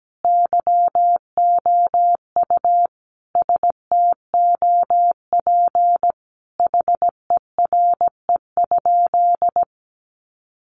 Morse code, heard as a keyed tone, about 17 wpm, 700 Hz. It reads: YOU STOP HERE?
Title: Sentences from Top 300 Words 17wpm